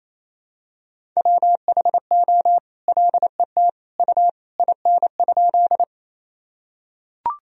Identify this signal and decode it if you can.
Morse code, heard as a keyed tone, about 28 wpm, 700 Hz. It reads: WHO LET U IN?
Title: Sentences from Top 300 Words and Abbreviations 28wpm